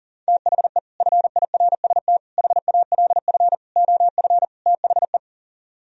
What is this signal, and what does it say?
Morse code, heard as a keyed tone, about 38 wpm, 700 Hz. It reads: THE FIRST HALF OF THE
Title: Speed-Racing - Sets of 5 Words 40wpm